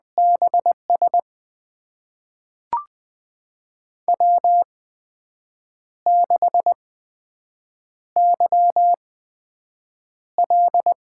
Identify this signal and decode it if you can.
Morse code, heard as a keyed tone, about 20 wpm, spaced as for 7 wpm, 700 Hz. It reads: BS W6YL
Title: USA Call Signs - 5 Characters - 8x Character Spacing - 20wpm